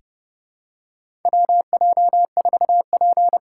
Morse code, heard as a keyed tone, about 29 wpm, 700 Hz. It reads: WJ4P
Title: USA Call Signs - 4 Characters - 4x Character Spacing - 30wpm